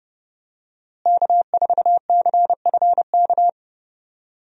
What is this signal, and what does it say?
Morse code, heard as a keyed tone, about 30 wpm, 700 Hz. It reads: K4CFK
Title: USA Call Signs - 5 Characters - Repeated 3x - 30wpm